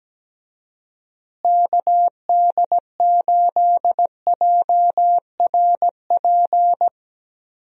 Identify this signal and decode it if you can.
Morse code, heard as a keyed tone, about 17 wpm, 700 Hz. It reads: KD8JRP
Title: USA Call Signs - 6 Characters - Repeated 3x - 17wpm